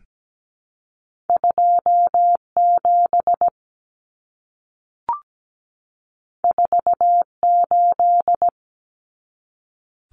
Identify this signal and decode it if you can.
Morse code, heard as a keyed tone, about 17 wpm, 700 Hz. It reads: 2 7 4 8
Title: Common Cut Numbers - 2-Digit 17wpm